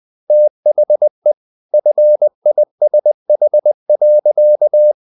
Morse code, heard as a keyed tone, about 20 wpm, 600 Hz. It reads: THE FISH.
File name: the old man and the sea part 2 20wpm